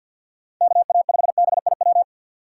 Morse code, heard as a keyed tone, about 47 wpm, 700 Hz. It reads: KA5BIW
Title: USA Call Signs - 6 Characters - 4x Character Spacing - 50wpm